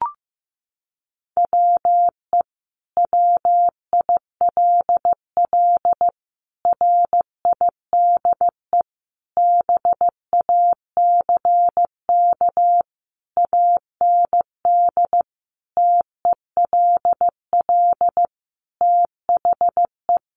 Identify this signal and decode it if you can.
Morse code, heard as a keyed tone, about 15 wpm, 700 Hz. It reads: WE WILL RIDE BACK AND TELL THE
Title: Sentences from Top 1200 Words 15wpm